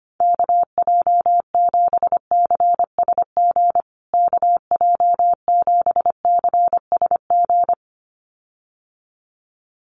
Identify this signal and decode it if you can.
Morse code, heard as a keyed tone, about 25 wpm, 700 Hz. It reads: KJ7CSG KJ7CSG
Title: USA Call Signs Repeated 2x 25wpm